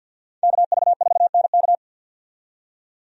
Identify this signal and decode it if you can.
Morse code, heard as a keyed tone, about 48 wpm, 700 Hz. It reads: KV4NK
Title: USA Call Signs - 5 Characters - 50wpm